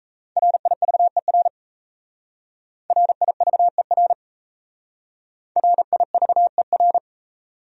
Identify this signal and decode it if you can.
Morse code, heard as a keyed tone, about 37 wpm, 700 Hz. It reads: RIVER RIVER RIVER